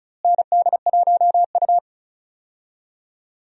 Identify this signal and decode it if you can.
Morse code, heard as a keyed tone, about 34 wpm, 700 Hz. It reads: ND1U
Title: USA Call Signs - 4 Characters - Repeated 2x - 35wpm